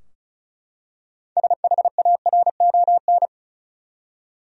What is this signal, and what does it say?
Morse code, heard as a keyed tone, about 36 wpm, 700 Hz. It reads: SHARON
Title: USA Common Names - 35wpm